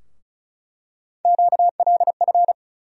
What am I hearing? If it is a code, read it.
Morse code, heard as a keyed tone, about 36 wpm, 700 Hz. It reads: QLF
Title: USA Call Sign Postfixes - 3 Letters 35wpm